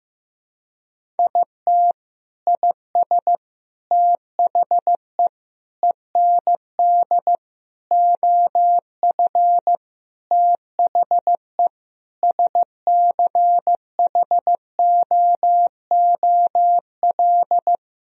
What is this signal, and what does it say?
Morse code, heard as a keyed tone, about 15 wpm, 700 Hz. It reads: IT IS THE END OF THE SCHOOL
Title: Sentences from Top 1000 Words - Review Part 1 - 15wpm